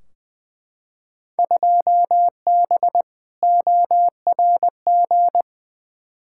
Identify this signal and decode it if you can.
Morse code, heard as a keyed tone, about 20 wpm, 700 Hz. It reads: 2B ORG
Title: Field Day Exchanges Encore - 20wpm